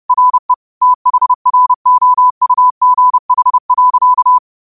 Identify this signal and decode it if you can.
Morse code, heard as a keyed tone, about 30 wpm, 1000 Hz. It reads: RE THROUGH.
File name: qotdfiles / 30wpm